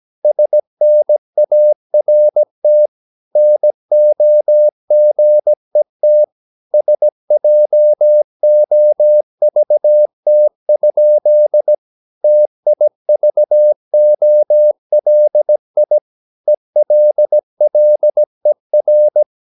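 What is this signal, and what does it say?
Morse code, heard as a keyed tone, about 17 wpm, 600 Hz. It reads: SNART NOGET SJOVT? TIVOLI ELLER
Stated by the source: Slow (QRS) Morse practice at 17 wpm, from CW med Gnister.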